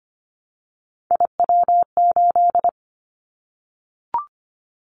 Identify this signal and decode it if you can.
Morse code, heard as a keyed tone, about 25 wpm, 700 Hz. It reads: IW8
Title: Callsign Prefixes for Top 25 DXCC - 25wpm